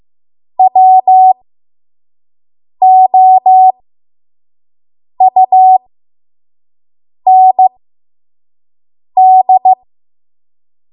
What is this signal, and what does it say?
Morse code, heard as a keyed tone, about 15 wpm, spaced as for 5 wpm, 750 Hz. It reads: WOUND
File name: Archive / 5 WPM / 05WPM